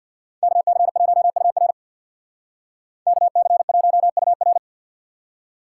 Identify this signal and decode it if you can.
Morse code, heard as a keyed tone, about 62 wpm, 700 Hz. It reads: KC1UR KC1UR
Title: Speed Racing - USA Call Signs 50wpm